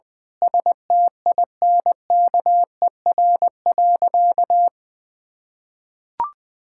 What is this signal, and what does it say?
Morse code, heard as a keyed tone, about 20 wpm, 700 Hz. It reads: STINKER.
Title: Pangram Challenge v2 - 20wpm